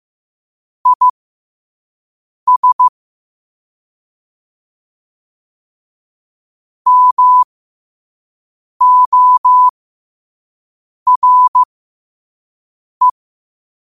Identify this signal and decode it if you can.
Morse code, heard as a keyed tone, about 15 wpm, spaced as for 5 wpm, 1000 Hz. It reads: IS MORE
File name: qotdfiles / 5wpm